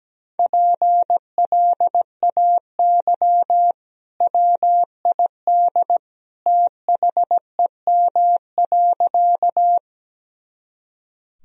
Morse code, heard as a keyed tone, about 17 wpm, 700 Hz. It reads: PLAY WID THEM.
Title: Sentences from Top 200 Words and Abbreviations 17wpm